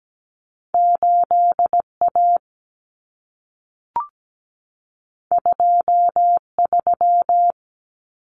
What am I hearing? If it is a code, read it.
Morse code, heard as a keyed tone, about 17 wpm, 700 Hz. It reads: 8A 23
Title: Common Cut Numbers - 2-Digit 17wpm